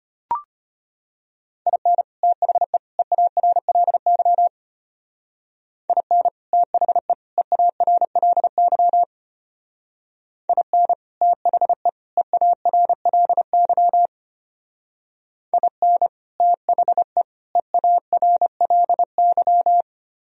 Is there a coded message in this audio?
Yes: Morse code from 1.7 to 19.8 s, heard as a keyed tone, about 31 wpm, 700 Hz. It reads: IN THE EARLY IN THE EARLY IN THE EARLY IN THE EARLY